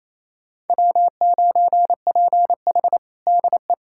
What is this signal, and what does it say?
Morse code, heard as a keyed tone, about 28 wpm, 700 Hz. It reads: W9PH DE
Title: USA Callsign DE Callsign - 4 Characters - 28wpm